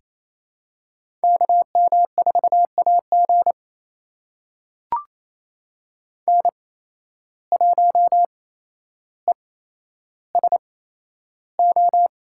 Morse code, heard as a keyed tone, about 28 wpm, spaced as for 12 wpm, 700 Hz. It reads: KM4AG N1ESO
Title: USA Call Signs - 8x Character Spacing - 28wpm